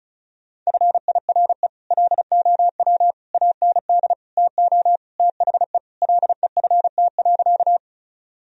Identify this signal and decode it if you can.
Morse code, heard as a keyed tone, about 34 wpm, 700 Hz. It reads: FIRE LOW AND TO THE LEFT.
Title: Sentences from Top 1000 Words - Review Part 1 - 35wpm